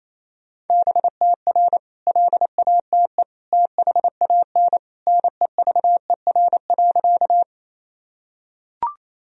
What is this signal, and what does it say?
Morse code, heard as a keyed tone, about 28 wpm, 700 Hz. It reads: BTR LATE THAN NEVER.
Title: Sentences from Top 400 Words and Abbreviations 28wpm